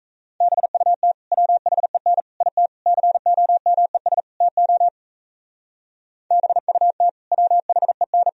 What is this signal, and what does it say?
Morse code, heard as a keyed tone, about 41 wpm, 700 Hz. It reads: BUT WHEN IT COMES TO BUT WHEN